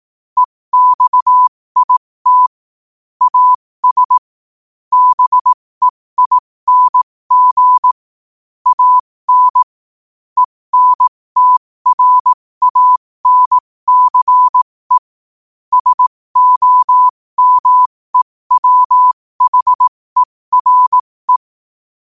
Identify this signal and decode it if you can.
Morse code, heard as a keyed tone, about 18 wpm, spaced as for 15 wpm, 1000 Hz. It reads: EXIT AS BEING AN ENTRANCE SOMEWHERE
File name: qotdfiles / 15wpm